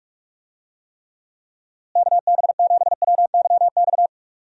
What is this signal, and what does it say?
Morse code, heard as a keyed tone, about 44 wpm, 700 Hz. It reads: KB7WYX
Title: USA Call Signs - 6 Characters - Repeated 2x - 45wpm